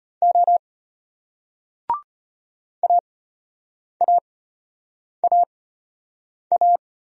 Morse code, heard as a keyed tone, about 33 wpm, spaced as for 8 wpm, 700 Hz. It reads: O AAAA